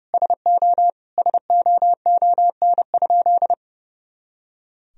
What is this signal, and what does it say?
Morse code, heard as a keyed tone, about 32 wpm, 700 Hz. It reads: SO SOON?
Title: Sentences from Top 400 Words and Abbreviations 30wpm